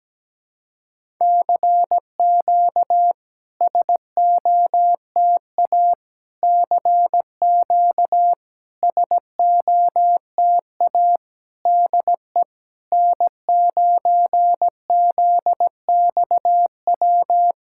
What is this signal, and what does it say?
Morse code, heard as a keyed tone, about 17 wpm, 700 Hz. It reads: CQ SOTA CQ SOTA DE N9ZXW
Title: SOTA QSO Practice 17wpm